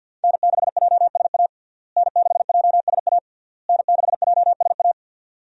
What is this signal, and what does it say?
Morse code, heard as a keyed tone, about 53 wpm, 700 Hz. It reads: N6JSA N6JSA N6JSA
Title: USA Call Signs - 5 Characters - Repeated 3x - 50wpm